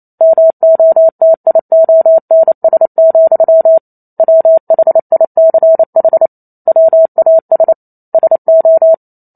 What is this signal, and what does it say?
Morse code, heard as a keyed tone, about 28 wpm, 650 Hz. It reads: MOTIONS, WHICH WAS SO
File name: Chapter 2 - 28 WPM